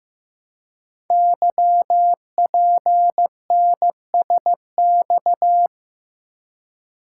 Morse code, heard as a keyed tone, about 15 wpm, 700 Hz. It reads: YPNSX
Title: Code Groups - 5 Letter:Numbers - 15wpm